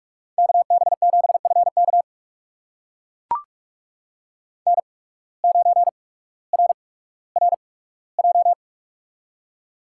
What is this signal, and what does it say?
Morse code, heard as a keyed tone, about 42 wpm, spaced as for 19 wpm, 700 Hz. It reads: KB7FK N9RRJ
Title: USA Call Signs - 5 Characters - 8x Character Spacing - 45wpm